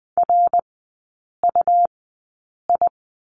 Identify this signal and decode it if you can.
Morse code, heard as a keyed tone, about 20 wpm, spaced as for 8 wpm, 700 Hz. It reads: RUI
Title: 107 Lesson 18e - Letter m - Three Characters 20wpm